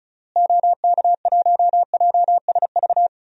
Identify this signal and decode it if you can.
Morse code, heard as a keyed tone, about 34 wpm, 700 Hz. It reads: OK1JSV
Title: DXCC-100 Call Signs - 6 Characters - Repeated 2x - 35wpm